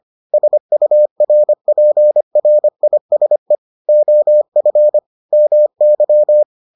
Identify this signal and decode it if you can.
Morse code, heard as a keyed tone, about 25 wpm, 600 Hz. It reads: SURPRISE OF MY